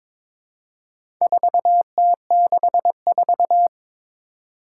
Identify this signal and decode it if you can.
Morse code, heard as a keyed tone, about 22 wpm, 700 Hz. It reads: 4T64
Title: Four Challenging Letter-Number Combinations 22wpm